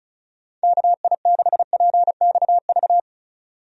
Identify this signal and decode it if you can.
Morse code, heard as a keyed tone, about 34 wpm, 700 Hz. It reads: KI6PXV
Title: USA Call Signs - 6 Characters - 35wpm